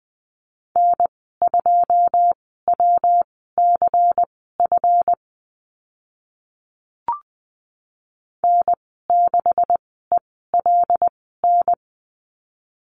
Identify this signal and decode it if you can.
Morse code, heard as a keyed tone, about 20 wpm, spaced as for 14 wpm, 700 Hz. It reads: N2WCF N6ELN
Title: 140 Lesson 23g - Letter f - Call Signs 20wpm